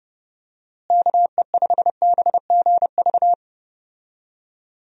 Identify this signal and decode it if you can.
Morse code, heard as a keyed tone, about 32 wpm, 700 Hz. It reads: KE5BGV
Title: USA Call Signs - 6 Characters - Repeated 3x - 30wpm